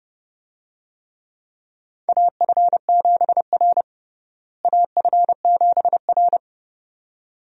Morse code, heard as a keyed tone, about 32 wpm, 700 Hz. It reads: AF7R AF7R
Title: USA Call Signs - 4 Characters - Repeated 2x - 30wpm